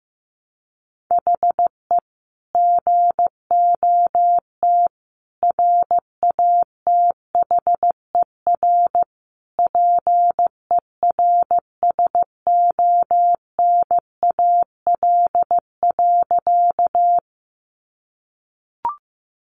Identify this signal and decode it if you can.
Morse code, heard as a keyed tone, about 15 wpm, 700 Hz. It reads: HE GOT RATHER PERSONAL.
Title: Sentences from Top 700 Words 15wpm